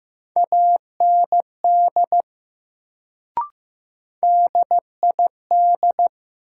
Morse code, heard as a keyed tone, about 16 wpm, 700 Hz. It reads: AND DID
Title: Sets of 2 Words - New Word Drill - 15wpm